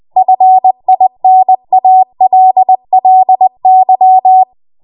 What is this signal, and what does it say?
Morse code, heard as a keyed tone, about 20 wpm, 750 Hz. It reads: FINALLY
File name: Morse / 20WPM